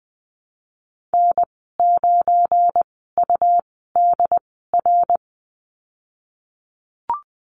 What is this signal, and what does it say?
Morse code, heard as a keyed tone, about 20 wpm, spaced as for 15 wpm, 700 Hz. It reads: N9UDR
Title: 219 Lesson 35g - Number 0 - Call Signs 20wpm